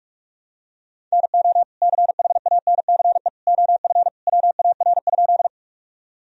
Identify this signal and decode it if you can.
Morse code, heard as a keyed tone, about 46 wpm, 700 Hz. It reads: NO CHANCE OF WAR?
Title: Sentences from Top 1000 Words - Review Part 2 - 45wpm